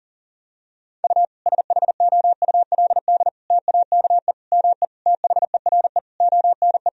Morse code, heard as a keyed tone, about 40 wpm, 700 Hz. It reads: U SHOULD TAKE ME THERE ONE
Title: Sentences from Top 300 Words and Abbreviations 40wpm